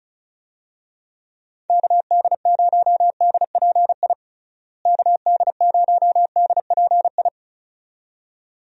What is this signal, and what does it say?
Morse code, heard as a keyed tone, about 34 wpm, 700 Hz. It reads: KD0DPI KD0DPI
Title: USA Call Signs - 6 Characters - Repeated 2x - 35wpm